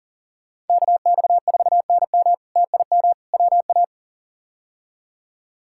Morse code, heard as a keyed tone, about 40 wpm, 700 Hz. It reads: KX4NM TIM WA
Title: K1USN SST QSO Practice - 40wpm